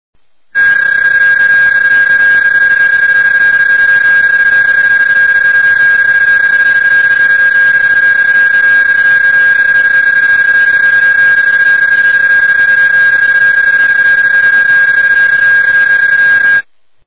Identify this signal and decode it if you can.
PSK31, about 31.25 baud.